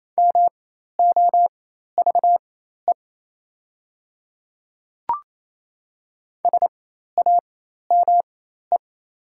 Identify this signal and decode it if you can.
Morse code, heard as a keyed tone, about 27 wpm, spaced as for 13 wpm, 690 Hz. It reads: MOVE SAME
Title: 4x Character Spacing - Top 200 Words - 28wpm